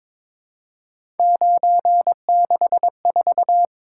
Morse code, heard as a keyed tone, about 22 wpm, 700 Hz. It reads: 964
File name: practice / 22wpm / Three Letter-Number Combinations - 22wpm